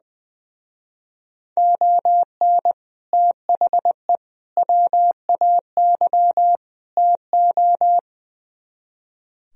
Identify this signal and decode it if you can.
Morse code, heard as a keyed tone, about 22 wpm, 700 Hz. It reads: ON THE WAY TO
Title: Speed-Racing - Sets of 4 Words 20wpm